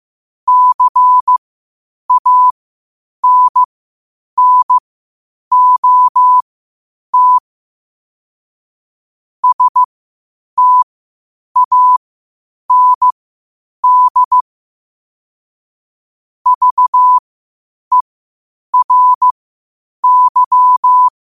Morse code, heard as a keyed tone, about 15 wpm, spaced as for 8 wpm, 1000 Hz. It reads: CANNOT STAND VERY
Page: Quotes for Thu, 14 Aug 2025 in Morse Code at 8 words per minute.